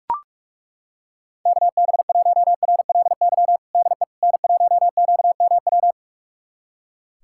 Morse code, heard as a keyed tone, about 41 wpm, 700 Hz. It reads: KB1RLY DE N1QMW
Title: USA Callsign DE Callsign 45wpm